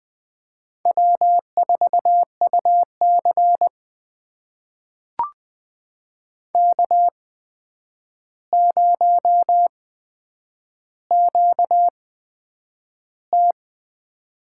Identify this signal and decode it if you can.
Morse code, heard as a keyed tone, about 20 wpm, spaced as for 9 wpm, 700 Hz. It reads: W4UC K0QT
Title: USA Call Signs - 4 Characters - 8x Character Spacing - 20wpm